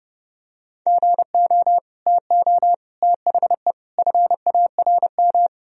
Morse code, heard as a keyed tone, about 31 wpm, 700 Hz. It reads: GO TO THE FARM
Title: Sentences from Top 1000 Words - Review Part 2 - 30wpm